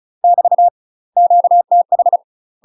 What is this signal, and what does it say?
Morse code, heard as a keyed tone, about 34 wpm, 700 Hz. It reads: <BT> QTH
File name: UK MASTER QSOs TRAINING / 35WPM / UK MASTER QSOs 0042